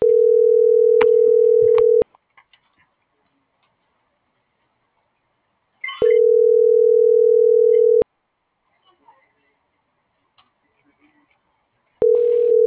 Ringback tone (North America), 440 + 480 Hz.